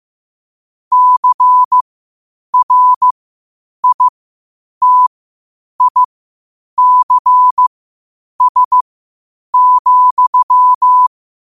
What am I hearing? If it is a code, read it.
Morse code, heard as a keyed tone, about 15 wpm, spaced as for 8 wpm, 1000 Hz. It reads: CRITICS,
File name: qotdfiles / 8wpm